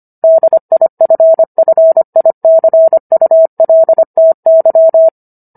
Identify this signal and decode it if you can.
Morse code, heard as a keyed tone, about 25 wpm, 650 Hz. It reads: DIFFICULTY